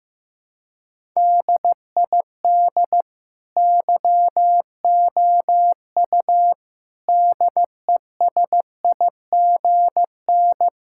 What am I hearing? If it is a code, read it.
Morse code, heard as a keyed tone, about 15 wpm, 700 Hz. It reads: DID YOU DESIGN
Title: Sentences from Top 1000 Words - Review Part 5 - 15wpm